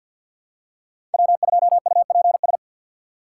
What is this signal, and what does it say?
Morse code, heard as a keyed tone, about 52 wpm, 700 Hz. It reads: W2UPS
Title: USA Call Signs - 5 Characters - 2x Character Spacing - 50wpm